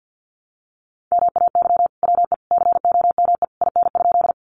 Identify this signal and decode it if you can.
Morse code, heard as a keyed tone, about 50 wpm, 700 Hz. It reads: MAY WE COME IN?
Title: Sentences from Top 100 Words 50wpm